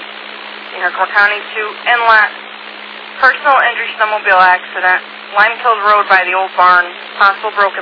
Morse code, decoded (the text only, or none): none